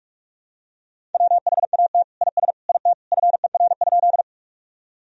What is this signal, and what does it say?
Morse code, heard as a keyed tone, about 46 wpm, 700 Hz. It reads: WHAT IS IT FER?
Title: Sentences from Top 100 Words and Abbreviations 45wpm